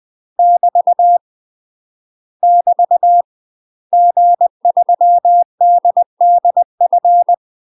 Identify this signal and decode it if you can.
Morse code, heard as a keyed tone, about 20 wpm, 700 Hz. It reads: <BT> <BT> G3DDF
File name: UK MASTER QSOs TRAINING / 20WPM / UK MASTER QSOs 0012